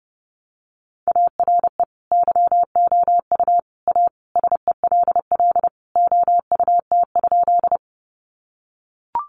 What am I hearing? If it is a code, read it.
Morse code, heard as a keyed tone, about 30 wpm, 700 Hz. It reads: ARE YOU A SELL OUT?
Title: Sentences from Top 1100 Words 30wpm